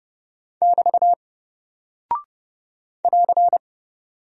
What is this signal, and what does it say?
Morse code, heard as a keyed tone, about 30 wpm, 700 Hz. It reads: <BT> <AR>
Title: Single Character (Letter:Number:Punctuation:Prosign) Alt 30wpm